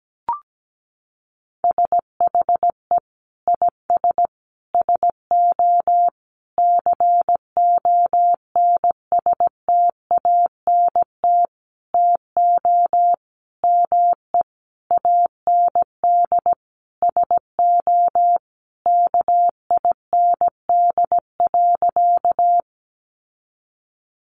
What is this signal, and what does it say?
Morse code, heard as a keyed tone, about 17 wpm, 700 Hz. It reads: SHE IS SO CONSTANT TO ME AND SO KIND.
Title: Sentences from Top 1500 Words 17wpm